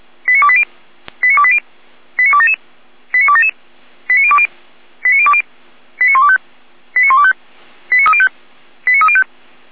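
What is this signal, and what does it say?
Five-tone selective call (ZVEI), heard as quick runs of five tones, tones of about 70 ms. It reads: 89280 89280 89277 89277 89020 89020 89125 89125 89305 89305
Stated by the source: ZVEI five-tone selective call received by SDR.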